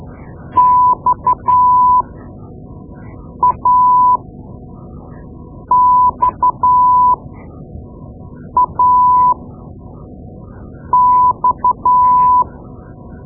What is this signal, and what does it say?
Morse code, heard as a keyed tone, about 12 wpm, spaced as for 5 wpm, 1000 Hz. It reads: XAXAX
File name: Morse Lessons / Morse Lessons / Tape-2B